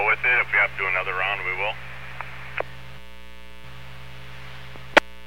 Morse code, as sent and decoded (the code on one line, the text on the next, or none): none